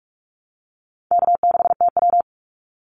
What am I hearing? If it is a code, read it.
Morse code, heard as a keyed tone, about 46 wpm, 700 Hz. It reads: K6TW
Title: NA SOTA Activator Callsigns 45wpm